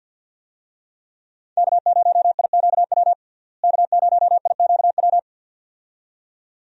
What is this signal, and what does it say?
Morse code, heard as a keyed tone, about 46 wpm, 700 Hz. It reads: K0IQW K0IQW
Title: USA Call Signs - 5 Characters - Repeated 2x - 50wpm